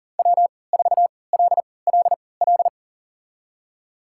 Morse code, heard as a keyed tone, about 40 wpm, spaced as for 23 wpm, 700 Hz. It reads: W4LLL
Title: USA Call Signs - 5 Characters - 3x Character Spacing - 40wpm